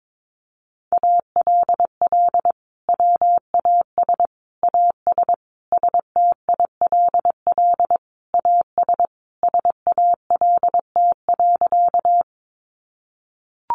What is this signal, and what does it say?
Morse code, heard as a keyed tone, about 22 wpm, 700 Hz. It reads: ALL WAS AS STILL AS SALT.
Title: Sentences from Top 800 Words 22wpm